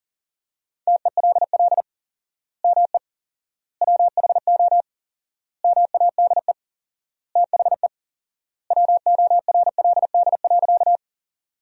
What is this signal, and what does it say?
Morse code, heard as a keyed tone, about 40 wpm, 700 Hz. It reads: TELL ME WHO MADE THE WORLD.
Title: 4x Word-Spacing - Sentences from Top 200 Words - 40wpm